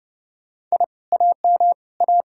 Morse code, heard as a keyed tone, about 40 wpm, 700 Hz. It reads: I AM A